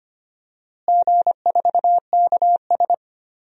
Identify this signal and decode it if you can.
Morse code, heard as a keyed tone, about 25 wpm, 700 Hz. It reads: G4KS
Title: DXCC-100 Call Signs - 4 Characters - Repeated 2x - 25wpm